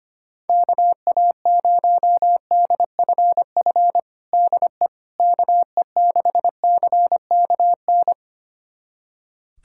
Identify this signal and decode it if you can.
Morse code, heard as a keyed tone, about 25 wpm, 700 Hz. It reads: KA0DFF DE KE6CKN
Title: USA Callsign DE Callsign 25wpm